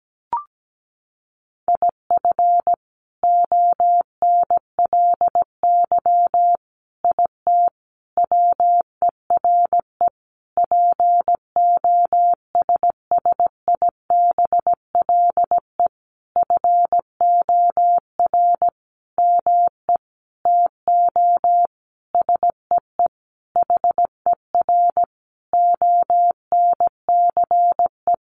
Morse code, heard as a keyed tone, about 17 wpm, 700 Hz. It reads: IF ONLY IT WERE POSSIBLE FOR ME TO SEE HER ONCE
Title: Sentences from Top 500 Words 17wpm